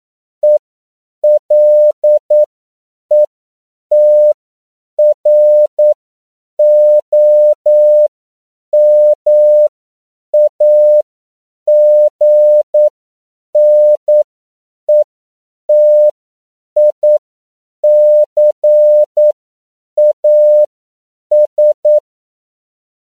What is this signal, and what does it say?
Morse code, heard as a keyed tone, about 9 wpm, spaced as for 7 wpm, 600 Hz. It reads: ELETROMAGNETICAS
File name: CW / 10PPM